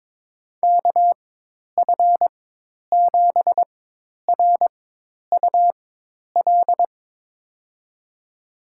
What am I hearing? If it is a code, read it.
Morse code, heard as a keyed tone, about 22 wpm, spaced as for 10 wpm, 700 Hz. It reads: KF7RUL
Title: USA Call Signs - 6 Characters - 4x Character Spacing - 22wpm